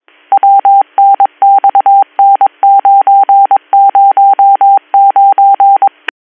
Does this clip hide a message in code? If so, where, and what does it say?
Yes: Morse code from 0.3 to 5.9 s, heard as a keyed tone, about 22 wpm, 790 Hz. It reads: WNXN909